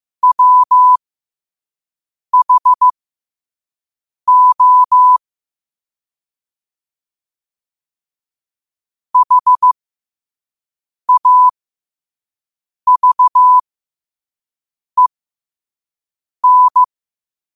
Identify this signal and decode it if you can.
Morse code, heard as a keyed tone, about 15 wpm, spaced as for 5 wpm, 1000 Hz. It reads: WHO HAVEN
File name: qotdfiles / 5wpm